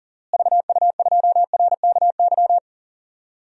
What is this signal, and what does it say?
Morse code, heard as a keyed tone, about 40 wpm, 700 Hz. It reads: VU2RKY
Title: DXCC-100 Call Signs - 6 Characters - Repeated 3x - 40wpm